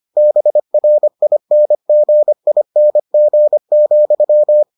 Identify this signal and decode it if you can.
Morse code, heard as a keyed tone, about 25 wpm, 600 Hz. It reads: BRINGING,